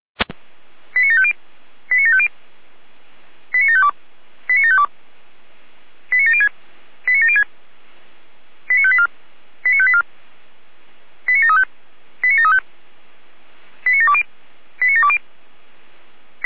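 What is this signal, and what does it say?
Five-tone selective call (ZVEI), heard as quick runs of five tones, tones of about 70 ms. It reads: 89750 89750 89752 89752 89706 89706 89584 89584 89735 89735 89720 89720
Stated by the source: Five-tone ZVEI selective call received by SDR, from audio.f85360.feuerwehr-orlamuende.org fire brigade (Feuerwehr Orlamünde) server.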